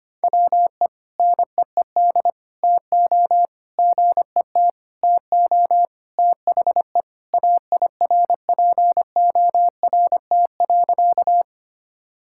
Morse code, heard as a keyed tone, about 25 wpm, 700 Hz. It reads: WE NEED TO GET TO THE AIRPORT.
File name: practice / 25wpm / Sentences with 3-Word Phrases - 25wpm